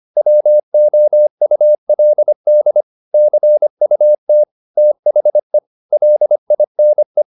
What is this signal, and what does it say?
Morse code, heard as a keyed tone, about 25 wpm, 600 Hz. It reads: WOULD CUT THE LINE
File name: the old man and the sea part 1 25wpm